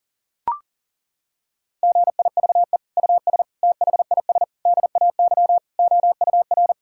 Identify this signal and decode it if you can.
Morse code, heard as a keyed tone, about 40 wpm, 700 Hz. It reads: GIVE US THIS DAY OUR
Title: Sentences from Top 1600 Words 40wpm